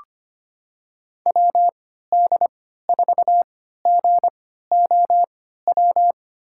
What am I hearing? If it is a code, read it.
Morse code, heard as a keyed tone, about 25 wpm, spaced as for 14 wpm, 700 Hz. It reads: WD4GOW
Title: USA Call Signs - 6 Characters - 3x Character Spacing - 25wpm